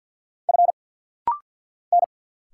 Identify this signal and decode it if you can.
Morse code, heard as a keyed tone, about 50 wpm, 700 Hz. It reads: F N
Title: Single Letter-Number Alt - Rapid-Fire 50wpm